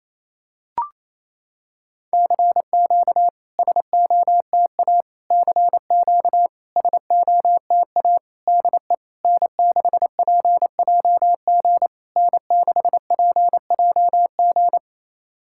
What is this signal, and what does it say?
Morse code, heard as a keyed tone, about 28 wpm, 690 Hz. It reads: CQ SOTA CQ SOTA DE N6PJG N6PJG
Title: SOTA QSO Practice 28wpm